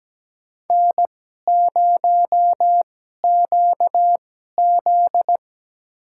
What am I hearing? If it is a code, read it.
Morse code, heard as a keyed tone, about 17 wpm, spaced as for 12 wpm, 700 Hz. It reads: N0QZ
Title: USA Call Signs - 4 Characters - 2x Character Spacing - 17wpm